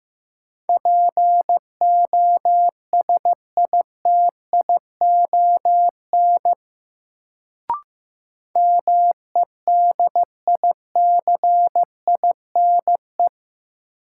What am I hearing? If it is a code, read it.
Morse code, heard as a keyed tone, about 15 wpm, 700 Hz. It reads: POSITION MEDICINE
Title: Sentences from Top 900 Words - New Word Drill - 15wpm